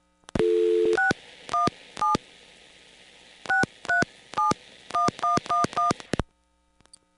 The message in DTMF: 617 6371111